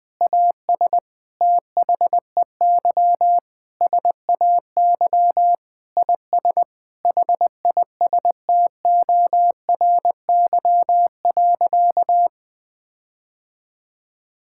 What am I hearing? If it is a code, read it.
Morse code, heard as a keyed tone, about 21 wpm, 700 Hz. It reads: AS THEY SAY IS HISTORY.
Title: Sentences with 8-Word Phrases - 20wpm